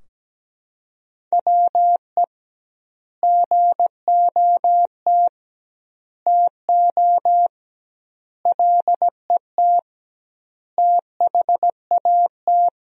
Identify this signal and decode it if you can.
Morse code, heard as a keyed tone, about 17 wpm, 700 Hz. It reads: WE GOT TO LET THAT
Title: 2x Word-Spacing - Sentences from Top 300 Words - 17wpm